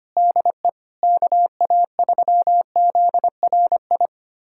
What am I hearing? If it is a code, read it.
Morse code, heard as a keyed tone, about 25 wpm, 700 Hz. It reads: DE KA3ZRI
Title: USA Callsign DE Callsign - 6 Characters - 25wpm